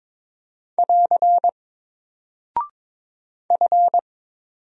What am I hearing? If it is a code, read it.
Morse code, heard as a keyed tone, about 22 wpm, 700 Hz. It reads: <AR> F
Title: Single Character (Letter:Number:Punctuation:Prosign) 22wpm